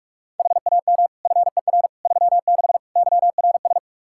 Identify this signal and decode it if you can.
Morse code, heard as a keyed tone, about 46 wpm, 700 Hz. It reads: HAM FER 36 YRS
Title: Practice QSO - Exchange 3 - More Info - 45wpm